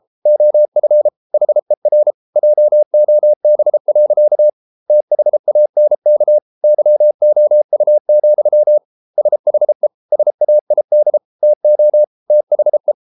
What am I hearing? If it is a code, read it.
Morse code, heard as a keyed tone, about 33 wpm, 600 Hz. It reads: OF HER JOB. THANK YOU, SHE SAID TO THE